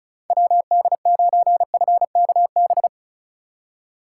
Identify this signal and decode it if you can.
Morse code, heard as a keyed tone, about 44 wpm, 700 Hz. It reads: WD9FKB